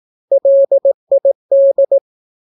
Morse code, heard as a keyed tone, about 18 wpm, 550 Hz. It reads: LID
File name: naukapodstaw / skroty / skroty